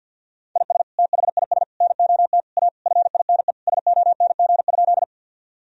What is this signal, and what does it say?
Morse code, heard as a keyed tone, about 50 wpm, 700 Hz. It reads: IS THIS NOT A FINE SONG?
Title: Sentences from Top 1000 Words - Review Part 2 - 50wpm